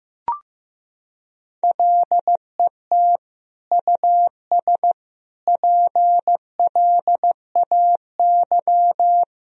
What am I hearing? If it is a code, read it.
Morse code, heard as a keyed tone, about 15 wpm, 700 Hz. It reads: LET US PLAY
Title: Sentences from Top 1000 Words - Review Part 3 - 15wpm